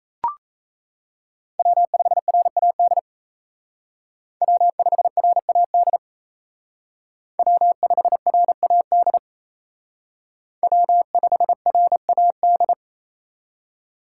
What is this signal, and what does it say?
Morse code, heard as a keyed tone, about 32 wpm, 700 Hz. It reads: W5RAD W5RAD W5RAD W5RAD
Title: Speed Racing - USA 1x3 Call Signs 28wpm